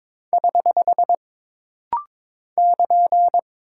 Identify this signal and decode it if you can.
Morse code, heard as a keyed tone, about 22 wpm, 700 Hz. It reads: <HH> <KN>
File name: practice / 22wpm / Single Punctuation-Prosign - Rapid-Fire - 22wpm